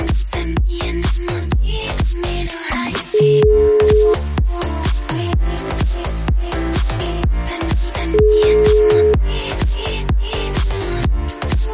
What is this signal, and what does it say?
Ringback tone (Europe, 425 Hz).